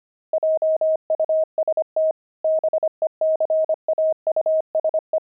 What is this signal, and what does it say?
Morse code, heard as a keyed tone, about 25 wpm, 630 Hz. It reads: JUST BECAUSE
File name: pangram-fs-25-cs-25-kidding